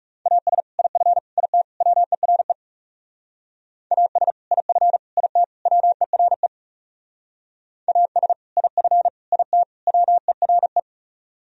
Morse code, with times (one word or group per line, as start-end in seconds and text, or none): 0.3-0.6 AS
0.8-1.2 IF
1.4-1.6 IT
1.8-2.5 WERE
3.9-4.3 AS
4.5-5.0 IF
5.2-5.4 IT
5.7-6.5 WERE
7.9-8.3 AS
8.6-9.1 IF
9.3-9.6 IT
9.9-10.8 WERE